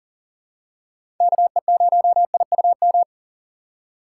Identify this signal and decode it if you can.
Morse code, heard as a keyed tone, about 40 wpm, 700 Hz. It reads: KE0IUM
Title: USA Call Signs - 6 Characters - 6x Character Spacing - 40wpm